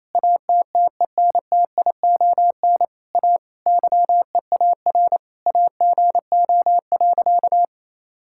Morse code, heard as a keyed tone, about 28 wpm, 700 Hz. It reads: ATTENTION A YEAR AGO.
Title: Sentences from Top 1000 Words - Review Part 6 - 28wpm